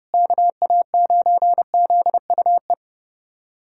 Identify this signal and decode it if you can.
Morse code, heard as a keyed tone, about 32 wpm, 700 Hz. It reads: KA9ZUE
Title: USA Callsign DE Callsign 30wpm